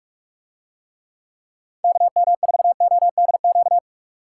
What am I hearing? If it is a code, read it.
Morse code, heard as a keyed tone, about 46 wpm, 700 Hz. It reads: KM4ODQ